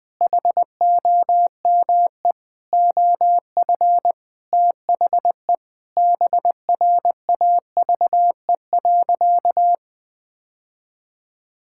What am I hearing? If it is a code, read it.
Morse code, heard as a keyed tone, about 20 wpm, 700 Hz. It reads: HOME OF THE BRAVE.
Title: Sentences with 9-Word Phrases - Spoken Beginning - 20wpm